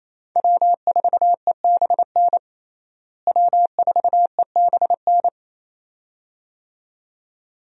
Morse code, heard as a keyed tone, about 28 wpm, 700 Hz. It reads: W4EBN W4EBN
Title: USA Call Signs - 5 Characters - Repeated 2x - 28wpm